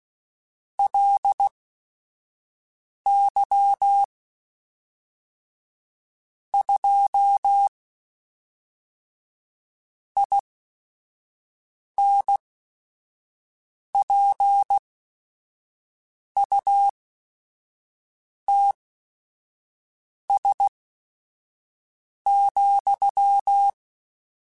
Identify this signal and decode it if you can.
Morse code, heard as a keyed tone, about 16 wpm, spaced as for 5 wpm, 800 Hz. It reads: LY 2 INPUTS,
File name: Technical Subjects / Technical Subjects-mp3s / AND Gate @5